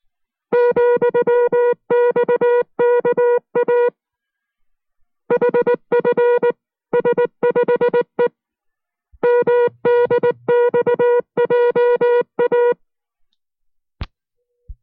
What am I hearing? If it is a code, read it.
Morse code, heard as a keyed tone, about 19 wpm, 480 Hz. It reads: ,XKA HFS5E MDXJA